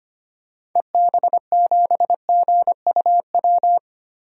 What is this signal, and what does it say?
Morse code, heard as a keyed tone, about 25 wpm, 700 Hz. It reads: EB7GUW
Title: DXCC-100 Call Signs - 6 Characters - Repeated 2x - 25wpm